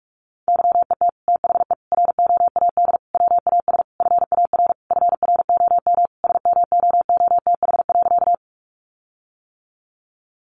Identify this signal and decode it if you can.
Morse code, heard as a keyed tone, about 44 wpm, 700 Hz. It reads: YET THE ROAD WAS FAR FROM SMOOTH.
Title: Sentences from Top 1400 Words 45wpm